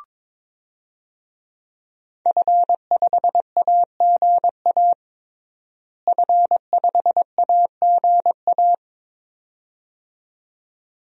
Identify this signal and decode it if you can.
Morse code, heard as a keyed tone, about 21 wpm, 700 Hz. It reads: F5AGA F5AGA
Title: DXCC-100 Call Signs - 5 Characters - Repeated 2x - 22wpm